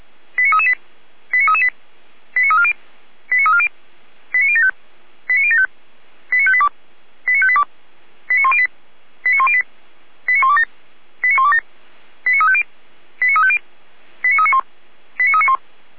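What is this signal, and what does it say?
Five-tone selective call (ZVEI), heard as quick runs of five tones, tones of about 70 ms. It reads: 89338 89338 89340 89340 89075 89075 89582 89582 89108 89108 89127 89127 89350 89350 89391 89391
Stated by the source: ZVEI five-tone call 89319 received on SDR, fire brigade Feuerwehr Orlamünde.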